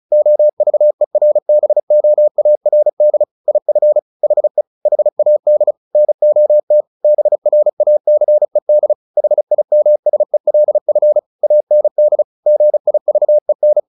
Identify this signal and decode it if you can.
Morse code, heard as a keyed tone, about 34 wpm, 600 Hz. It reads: OVERBOARD IF HE HAD NOT BRACED HIMSELF AND GIVEN
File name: the old man and the sea / the old man and the sea 1 35wpm